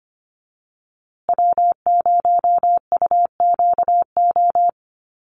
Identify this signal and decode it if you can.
Morse code, heard as a keyed tone, about 25 wpm, 700 Hz. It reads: W0UQO
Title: USA Call Signs Repeated 2x 25wpm